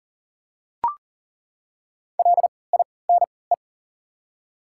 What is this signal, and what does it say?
Morse code, heard as a keyed tone, about 40 wpm, spaced as for 23 wpm, 700 Hz. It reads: LINE
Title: 3x Character Spacing - Top 200 Words - 40wpm